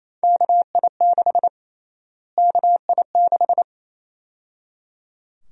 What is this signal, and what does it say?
Morse code, heard as a keyed tone, about 28 wpm, 700 Hz. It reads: KI6 KI6
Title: USA Call Sign Prefixes - Repeated 2x - 28wpm